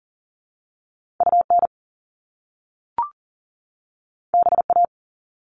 Morse code, heard as a keyed tone, about 41 wpm, 700 Hz. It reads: UN BA
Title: Top Two Letter Combinations in Words 40wpm